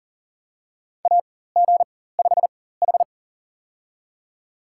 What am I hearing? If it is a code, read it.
Morse code, heard as a keyed tone, about 40 wpm, spaced as for 19 wpm, 700 Hz. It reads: AG5H
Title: USA Call Signs - 4 Characters - 4x Character Spacing - 40wpm